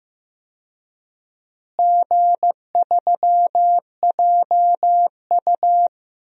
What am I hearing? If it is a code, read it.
Morse code, heard as a keyed tone, about 15 wpm, 700 Hz. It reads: G3JU
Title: DXCC-100 Call Signs - 4 Characters - Repeated 3x - 15wpm